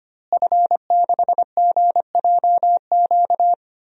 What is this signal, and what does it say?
Morse code, heard as a keyed tone, about 25 wpm, 700 Hz. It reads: F6GJQ
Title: DXCC-100 Call Signs - 5 Characters - Repeated 2x - 25wpm